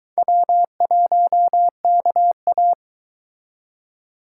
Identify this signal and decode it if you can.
Morse code, heard as a keyed tone, about 23 wpm, 700 Hz. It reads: W1KA